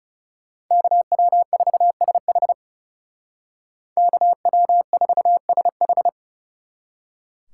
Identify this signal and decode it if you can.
Morse code, heard as a keyed tone, about 40 wpm, 700 Hz. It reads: KW4SH KW4SH